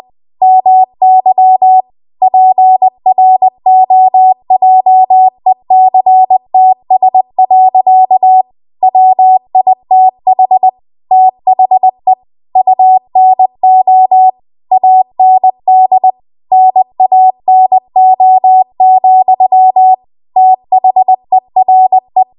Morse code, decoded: MY PROJECTS. WITH THE UNO AND NANO, THERE